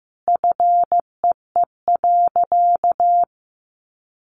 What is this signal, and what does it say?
Morse code, heard as a keyed tone, about 15 wpm, 700 Hz. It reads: FEE.
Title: Sentences from Top 1500 Words 15wpm